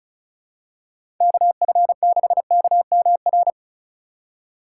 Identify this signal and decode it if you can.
Morse code, heard as a keyed tone, about 36 wpm, 700 Hz. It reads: KF6KMR
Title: USA Call Signs - 6 Characters - 3x Character Spacing - 35wpm